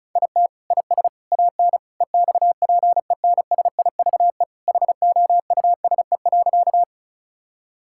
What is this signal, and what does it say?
Morse code, heard as a keyed tone, about 36 wpm, 700 Hz. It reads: IT IS AN EXPENSIVE HOUSE.
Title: Sentences from Top 1000 Words - Review Part 6 - 35wpm